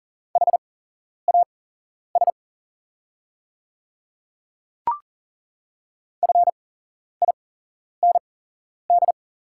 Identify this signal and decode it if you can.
Morse code, heard as a keyed tone, about 39 wpm, spaced as for 11 wpm, 690 Hz. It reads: HAS FIND
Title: 8x Character Spacing - Top 200 Words - 40wpm